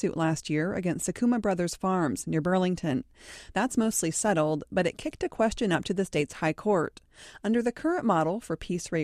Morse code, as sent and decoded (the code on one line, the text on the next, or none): none